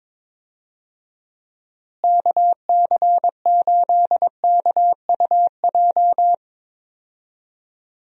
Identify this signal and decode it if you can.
Morse code, heard as a keyed tone, about 22 wpm, 700 Hz. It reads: KC8KUJ
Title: USA Call Signs - 6 Characters - Repeated 3x - 22wpm